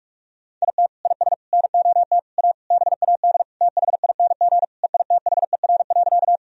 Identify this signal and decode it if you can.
Morse code, heard as a keyed tone, about 46 wpm, 700 Hz. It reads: IT IS NOT A BAD THING EITHER.
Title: Sentences from Top 1000 Words - Review Part 4 - 45wpm